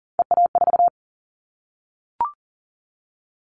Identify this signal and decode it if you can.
Morse code, heard as a keyed tone, about 40 wpm, 700 Hz. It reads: EA4
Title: Callsign Prefixes for Top 25 DXCC - Repeated 3x - 40wpm